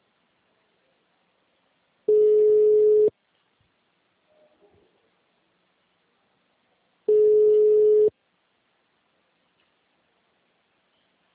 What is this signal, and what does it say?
Ringback tone (Europe, 425 Hz).